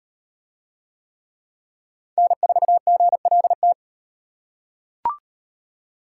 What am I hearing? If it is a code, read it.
Morse code, heard as a keyed tone, about 37 wpm, 700 Hz. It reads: N4GLT